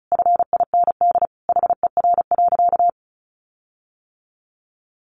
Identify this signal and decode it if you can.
Morse code, heard as a keyed tone, about 34 wpm, 700 Hz. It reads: FIND HER.
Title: Sentences from Top 1400 Words 35wpm